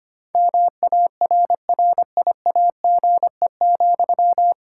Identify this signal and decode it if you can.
Morse code, heard as a keyed tone, about 25 wpm, 700 Hz. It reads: MARRIAGE,
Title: Sentences from Top 1000 Words - Review Part 5 - 25wpm